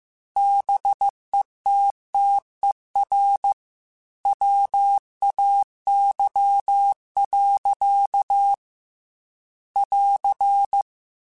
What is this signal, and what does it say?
Morse code, heard as a keyed tone, about 15 wpm, 800 Hz. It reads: BETTER WAY. <AR>
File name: Technical Subjects / Technical Subjects-mp3s / Gates Intro @13